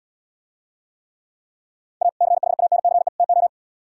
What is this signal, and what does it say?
Morse code, heard as a keyed tone, about 77 wpm, 700 Hz. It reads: A CHANCE TO